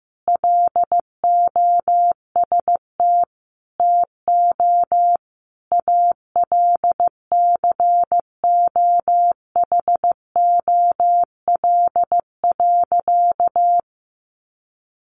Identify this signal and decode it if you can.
Morse code, heard as a keyed tone, about 15 wpm, 700 Hz. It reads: LOST TO ALCOHOL.
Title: Sentences from Top 1400 Words 15wpm